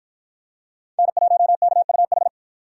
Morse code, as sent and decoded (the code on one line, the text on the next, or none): -. .---- -.- ..- ....
N1KUH